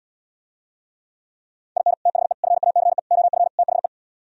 Morse code, heard as a keyed tone, about 77 wpm, 700 Hz. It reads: IN THE FACE OF THE